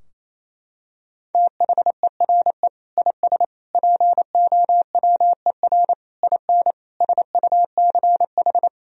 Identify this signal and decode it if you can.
Morse code, heard as a keyed tone, about 28 wpm, 700 Hz. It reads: THERE IS POWER IN SUCH
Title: Sentences from Top 1000 Words - Review Part 4 - 28wpm